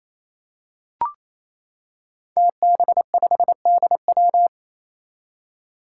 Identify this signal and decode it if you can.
Morse code, heard as a keyed tone, about 27 wpm, 700 Hz. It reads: TB5DW
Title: Five Challenging Letter-Number Combinations 28wpm